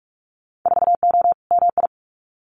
Morse code, heard as a keyed tone, about 46 wpm, 690 Hz. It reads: 4O MI
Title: Winter Field Day Exchanges Encore 45wpm